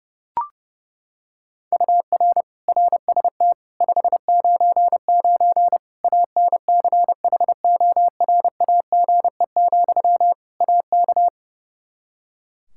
Morse code, heard as a keyed tone, about 30 wpm, 700 Hz. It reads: UR RST 599 ANCHORAGE, AK
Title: K1USN SST QSO Practice 30wpm